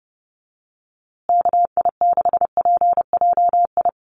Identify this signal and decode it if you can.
Morse code, heard as a keyed tone, about 30 wpm, 700 Hz. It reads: KI6PJI
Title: USA Call Signs Repeated 3x 30wpm